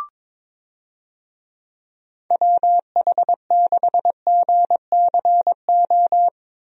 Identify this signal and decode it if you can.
Morse code, heard as a keyed tone, about 22 wpm, 700 Hz. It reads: WH6GCO
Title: USA Call Signs - 6 Characters - Repeated 3x - 22wpm